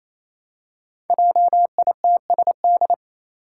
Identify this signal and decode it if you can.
Morse code, heard as a keyed tone, about 28 wpm, 700 Hz. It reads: JITSD
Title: Five Challenging Letter-Number Combinations 28wpm